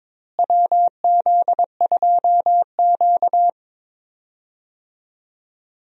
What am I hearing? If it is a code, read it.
Morse code, heard as a keyed tone, about 22 wpm, 700 Hz. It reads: WZ2Q
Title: USA Call Signs - 4 Characters - Repeated 2x - 22wpm